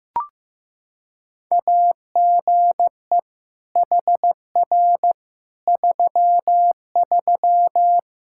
Morse code, heard as a keyed tone, about 15 wpm, 700 Hz. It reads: AGE HR 33
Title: K1USN SST QSO Practice 15wpm